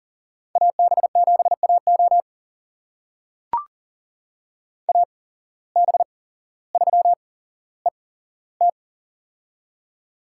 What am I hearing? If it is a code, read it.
Morse code, heard as a keyed tone, about 40 wpm, spaced as for 17 wpm, 700 Hz. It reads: AB7AO AB3ET
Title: USA Call Signs - 5 Characters - 8x Character Spacing - 40wpm